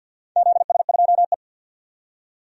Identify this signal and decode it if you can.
Morse code, heard as a keyed tone, about 52 wpm, 700 Hz. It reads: ZS2E
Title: DXCC-100 Call Signs - 4 Characters - 50wpm